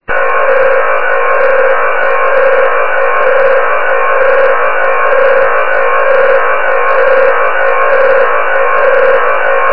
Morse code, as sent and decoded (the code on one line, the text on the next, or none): none